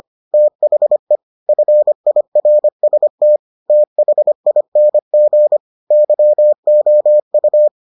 Morse code, heard as a keyed tone, about 25 wpm, 600 Hz. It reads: THE FIRST THING YOU